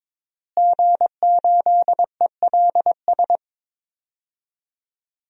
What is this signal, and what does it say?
Morse code, heard as a keyed tone, about 22 wpm, 700 Hz. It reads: G8ELS
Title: DXCC-100 Call Signs - 5 Characters - Repeated 2x - 22wpm